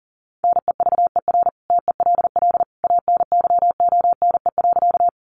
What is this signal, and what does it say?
Morse code, heard as a keyed tone, about 40 wpm, 700 Hz. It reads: NEVER TELL ANYONE.
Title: Sentences from Top 900 Words 40wpm